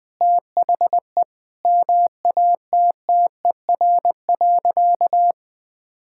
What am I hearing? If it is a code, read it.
Morse code, heard as a keyed tone, about 20 wpm, 700 Hz. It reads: THE MATTER.
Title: Sentences from Top 1000 Words - Review Part 5 - 20wpm